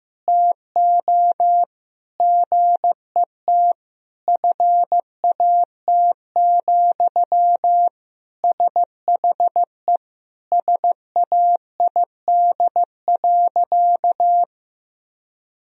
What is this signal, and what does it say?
Morse code, heard as a keyed tone, about 16 wpm, 700 Hz. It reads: TO GET FAT, SHE SAID.
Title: Sentences from Top 1000 Words - Review Part 4 - 15wpm